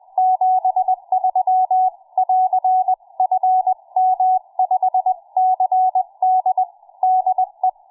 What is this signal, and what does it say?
Morse code, heard as a keyed tone, about 20 wpm, 740 Hz. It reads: 73<AR>FM5CD DE
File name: cw forum / fm5cd 1833 le17avril08 06h51